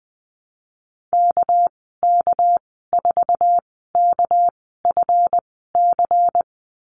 Morse code, heard as a keyed tone, about 20 wpm, spaced as for 14 wpm, 700 Hz. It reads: KK4KFC
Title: 207 Lesson 33g - Letter j - Call Signs 20wpm